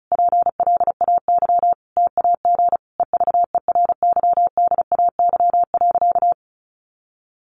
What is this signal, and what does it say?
Morse code, heard as a keyed tone, about 34 wpm, 700 Hz. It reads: PLAY TAG EVERYDAY.
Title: Additional Easy Phrases 35wpm